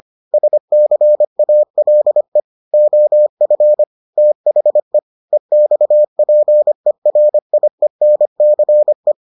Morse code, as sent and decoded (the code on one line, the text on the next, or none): ... -.-. .- .-.. / --- ..-. / - .... / . -..- .--. .-. .. -. -.-. .
SCALE OF THE EXPERIENCE